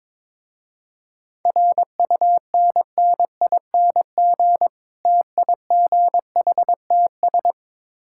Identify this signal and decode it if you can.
Morse code, heard as a keyed tone, about 22 wpm, 700 Hz. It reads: RUNNING TIGHTS
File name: Difficult Words with Hints - 22wpm